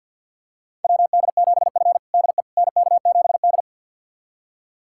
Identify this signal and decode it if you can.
Morse code, heard as a keyed tone, about 49 wpm, 700 Hz. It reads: WD7F DE NK7D
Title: USA Callsign DE Callsign - 4 Characters - 50wpm